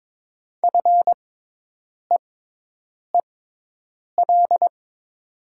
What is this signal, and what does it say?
Morse code, heard as a keyed tone, about 22 wpm, spaced as for 8 wpm, 700 Hz. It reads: FEEL
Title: 6x Character Spacing - Top 500 Words - 22wpm